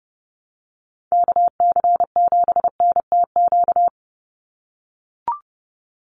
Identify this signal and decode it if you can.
Morse code, heard as a keyed tone, about 30 wpm, 700 Hz. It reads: KC7NTQ
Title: USA Call Signs 30wpm